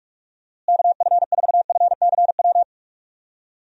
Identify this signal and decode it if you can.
Morse code, heard as a keyed tone, about 41 wpm, 700 Hz. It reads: KF4FCW
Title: USA Call Signs - 6 Characters - Repeated 3x - 45wpm